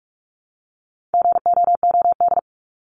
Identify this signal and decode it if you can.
Morse code, heard as a keyed tone, about 44 wpm, 700 Hz. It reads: GOOD
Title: Top 300 Words 45wpm